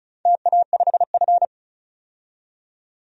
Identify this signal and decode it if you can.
Morse code, heard as a keyed tone, about 36 wpm, 700 Hz. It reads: TA5F
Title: DXCC-100 Call Signs - 4 Characters - 35wpm